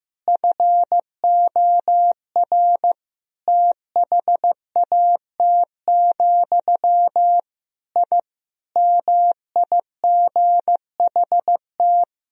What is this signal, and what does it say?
Morse code, heard as a keyed tone, about 15 wpm, 700 Hz. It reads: FOR THAT, I MIGHT
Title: Sentences from Top 1000 Words - Review Part 3 - 15wpm